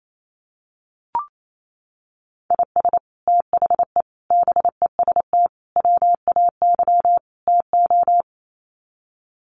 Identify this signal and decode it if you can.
Morse code, heard as a keyed tone, about 28 wpm, 700 Hz. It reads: IS THE BEST WAY TO
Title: Sets of 5 Words Encore 28wpm